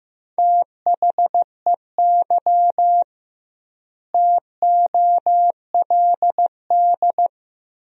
Morse code, heard as a keyed tone, about 15 wpm, 700 Hz. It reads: THEY TOLD
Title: 2x Word-Spacing - Sentences from Top 400 Words - 15wpm